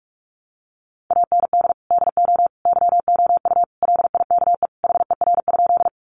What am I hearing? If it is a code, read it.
Morse code, heard as a keyed tone, about 44 wpm, 700 Hz. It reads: AND DO YOU LIKE HER?
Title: Sentences from Top 100 Words 45wpm